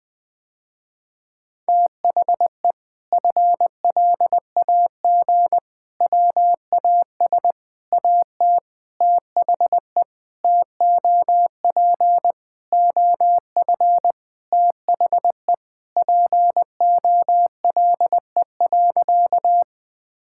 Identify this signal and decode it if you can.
Morse code, heard as a keyed tone, about 21 wpm, 700 Hz. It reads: THE FLAG WAS AT THE TOP OF THE POLE.
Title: Sentences with 5-Word Phrases - Spoken Beginning - 20wpm